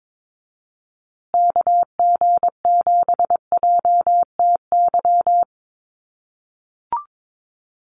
Morse code, heard as a keyed tone, about 22 wpm, 700 Hz. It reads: KG7JTY
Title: USA Call Signs Repeated 2x 22wpm